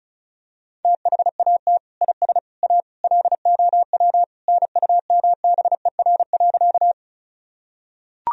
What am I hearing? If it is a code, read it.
Morse code, heard as a keyed tone, about 36 wpm, 700 Hz. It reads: THAT IS A LOW NUMBER.
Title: Sentences from Top 1000 Words - Review Part 1 - 35wpm